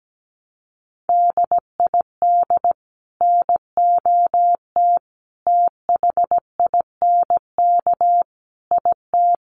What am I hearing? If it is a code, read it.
Morse code, heard as a keyed tone, about 17 wpm, 700 Hz. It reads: DID NOT THINK IT
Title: Sets of 5 Words Encore 17wpm